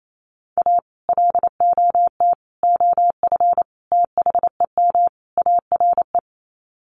Morse code, heard as a keyed tone, about 28 wpm, 700 Hz. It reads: A LOT OF THEM ARE
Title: Sets of 5 Words Encore2 28wpm